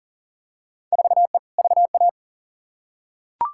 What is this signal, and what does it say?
Morse code, heard as a keyed tone, about 37 wpm, 700 Hz. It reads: 4E VA
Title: Field Day Exchanges Encore - 40wpm